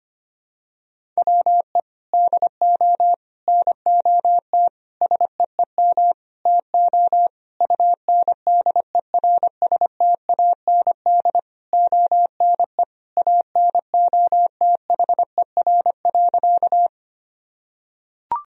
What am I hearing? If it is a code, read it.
Morse code, heard as a keyed tone, about 25 wpm, 700 Hz. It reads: WE DO NOT SEEM TO UNDERSTAND ONE ANOTHER.
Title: Sentences from Top 1000 Words - Review Part 1 - 25wpm